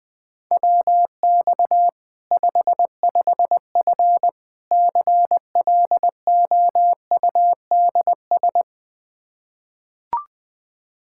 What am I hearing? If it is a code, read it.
Morse code, heard as a keyed tone, about 20 wpm, 700 Hz. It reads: WX 55F CLOUDS
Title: Practice QSO - Exchange 2 - Info - 20wpm